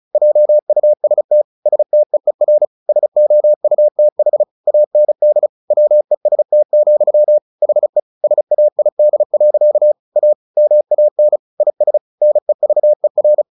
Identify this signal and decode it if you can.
Morse code, heard as a keyed tone, about 34 wpm, 600 Hz. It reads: JUST STEER SOUTH AND WEST, HE SAID. A MAN IS NEVER
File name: the old man and the sea 2 35wpm